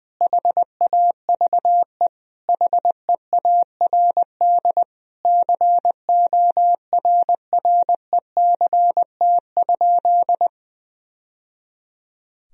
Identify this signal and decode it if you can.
Morse code, heard as a keyed tone, about 20 wpm, 700 Hz. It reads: HAVE HEARD CORRECT?
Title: Sentences from Top 1000 Words - Review Part 2 - 20wpm